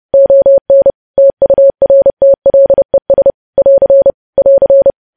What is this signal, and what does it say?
Morse code, heard as a keyed tone, about 30 wpm, 550 Hz. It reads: ON TURTLES <AR> <AR>
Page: News Headlines in Morse Code at 30 WPM